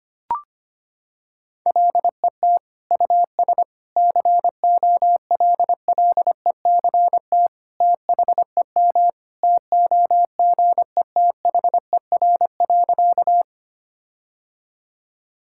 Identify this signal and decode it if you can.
Morse code, heard as a keyed tone, about 25 wpm, 700 Hz. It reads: LET US COLLECT THEM TOGETHER.
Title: Sentences from Top 1600 Words 25wpm